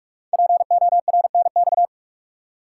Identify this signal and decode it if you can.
Morse code, heard as a keyed tone, about 44 wpm, 690 Hz. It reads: PORNX
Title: Code Groups - 5 Letters - 45wpm